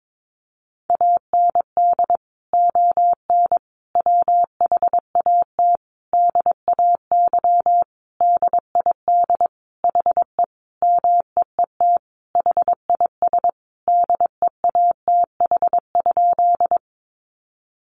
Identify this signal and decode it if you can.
Morse code, heard as a keyed tone, about 22 wpm, 700 Hz. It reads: AND ON WHAT DAY DID HE MEET HIS DEATH?
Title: Sentences from Top 1100 Words 22wpm